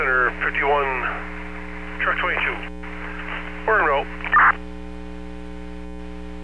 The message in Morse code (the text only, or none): none